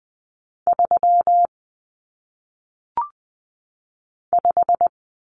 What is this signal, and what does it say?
Morse code, heard as a keyed tone, about 20 wpm, 700 Hz. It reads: 3 5
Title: Numbers - 1-Digit 20wpm